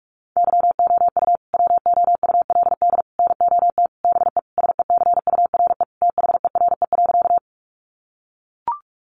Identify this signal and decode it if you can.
Morse code, heard as a keyed tone, about 44 wpm, 700 Hz. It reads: YOU WOULD NOT BE SECURE THERE.
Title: Sentences from Top 1200 Words 45wpm